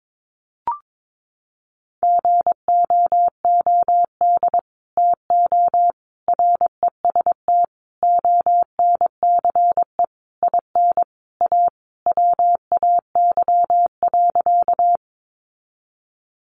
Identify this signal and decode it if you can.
Morse code, heard as a keyed tone, about 22 wpm, 700 Hz. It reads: GOOD TO REST ONCE IN A WAY.
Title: Sentences from Top 500 Words 22wpm